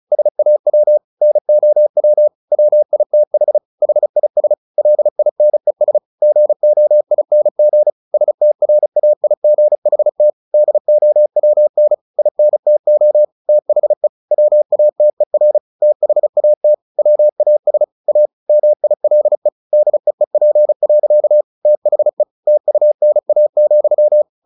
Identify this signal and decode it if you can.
Morse code, heard as a keyed tone, about 34 wpm, 600 Hz. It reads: SAW NOW WITH HIS LINES GOING STRAIGHT DOWN INTO THE WATER THAT WAS A MILE DEEP. THE TUNA,
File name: the old man and the sea / the old man and the sea 1 35wpm